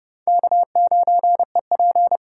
Morse code, heard as a keyed tone, about 27 wpm, 700 Hz. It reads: K9EP